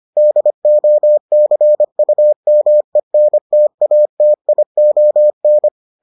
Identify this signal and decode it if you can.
Morse code, heard as a keyed tone, about 25 wpm, 600 Hz. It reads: DOCUMENTATION